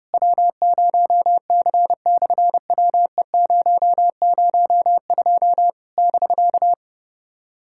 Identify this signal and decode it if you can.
Morse code, heard as a keyed tone, about 30 wpm, 700 Hz. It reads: W0C/WE002 <BK>
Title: SOTA QSO Practice 30wpm